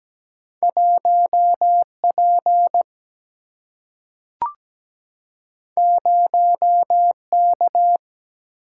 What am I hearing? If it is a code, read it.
Morse code, heard as a keyed tone, about 17 wpm, 700 Hz. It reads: 1 P 0 K
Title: Single-Letter Single-Number 17wpm